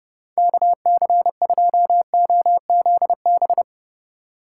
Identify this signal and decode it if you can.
Morse code, heard as a keyed tone, about 37 wpm, 700 Hz. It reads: KC2OZB
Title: Speed Racing - USA 2x3 Call Signs 30wpm